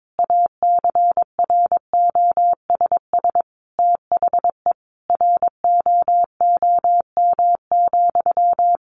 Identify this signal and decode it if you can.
Morse code, heard as a keyed tone, about 22 wpm, 700 Hz. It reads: ACROSS THE ROOM,
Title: A Sound Of Thunder 22wpm